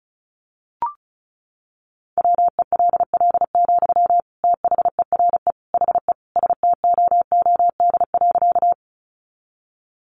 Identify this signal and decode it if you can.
Morse code, heard as a keyed tone, about 34 wpm, 700 Hz. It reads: WELL, THERE HE STOOD.
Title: Sentences from Top 500 Words 35wpm